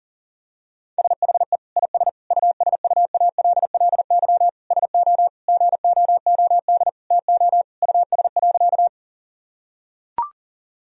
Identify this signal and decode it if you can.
Morse code, heard as a keyed tone, about 40 wpm, 700 Hz. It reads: SHE IS USUALLY SO GOOD TO US.
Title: Sentences from Top 1000 Words - Review Part 3 - 40wpm